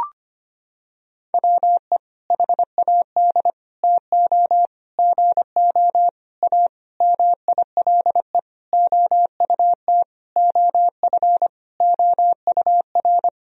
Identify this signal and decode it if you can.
Morse code, heard as a keyed tone, about 25 wpm, 700 Hz. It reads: WE HAD TO GO A MILE OUT OF OUR
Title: Sentences from Top 1000 Words - Review Part 1 - 25wpm